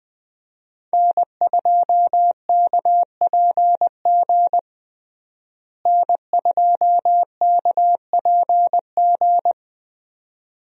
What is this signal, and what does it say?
Morse code, heard as a keyed tone, about 20 wpm, 700 Hz. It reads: N2KPG N2KPG
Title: USA Call Signs - 5 Characters - Repeated 2x - 20wpm